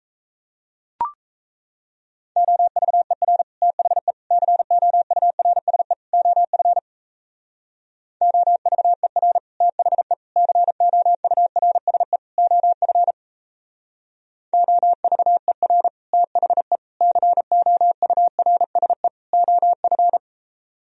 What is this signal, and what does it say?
Morse code, heard as a keyed tone, about 34 wpm, 700 Hz. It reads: OVER THE COURSE OF OVER THE COURSE OF OVER THE COURSE OF
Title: Speed-Racing - Sets of 4 Words 28wpm